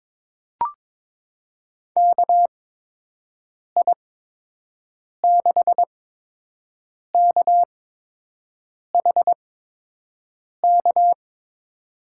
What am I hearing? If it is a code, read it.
Morse code, heard as a keyed tone, about 22 wpm, spaced as for 6 wpm, 700 Hz. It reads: KI6KHK
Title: USA Call Signs - 6 Characters - 8x Character Spacing - 22wpm